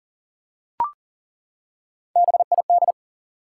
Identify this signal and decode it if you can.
Morse code, heard as a keyed tone, about 40 wpm, 700 Hz. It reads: BID
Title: Three Letter Words - 40wpm